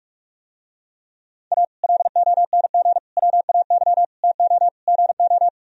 Morse code, heard as a keyed tone, about 47 wpm, 700 Hz. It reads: A LONG WAY TO GO